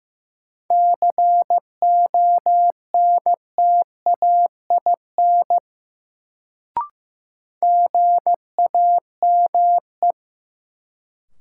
Morse code, heard as a keyed tone, about 15 wpm, 700 Hz. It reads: CONTAIN GAME
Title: Sentences from Top 500 Words - New Word Drill - 15wpm